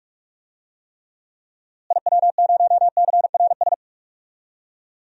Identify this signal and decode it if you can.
Morse code, heard as a keyed tone, about 46 wpm, 700 Hz. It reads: IW0CRS
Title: DXCC-100 Call Signs - 6 Characters - Repeated 3x - 45wpm